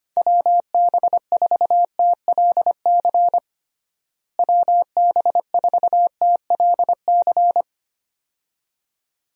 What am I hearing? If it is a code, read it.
Morse code, heard as a keyed tone, about 25 wpm, 700 Hz. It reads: WB4TLC WB4TLC
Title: USA Call Signs - 6 Characters - Repeated 2x - 25wpm